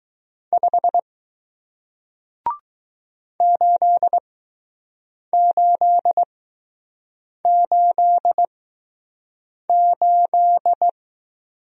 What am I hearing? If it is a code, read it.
Morse code, heard as a keyed tone, about 21 wpm, spaced as for 6 wpm, 700 Hz. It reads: 5 8888